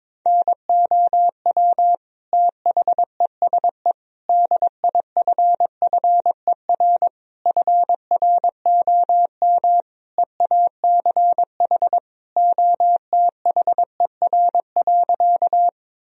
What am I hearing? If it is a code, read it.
Morse code, heard as a keyed tone, about 22 wpm, 700 Hz. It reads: NOW THESE DIFFER FROM EACH OTHER.
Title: Sentences from Top 1000 Words - Review Part 1 - 22wpm